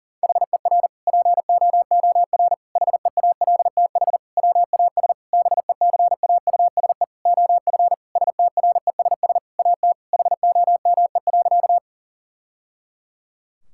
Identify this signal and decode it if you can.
Morse code, heard as a keyed tone, about 40 wpm, 700 Hz. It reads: HER POOR HEALTH WAS BECAUSE OF STRESS AT HOME.
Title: Sentences from Top 1000 Words - Review Part 6 - 40wpm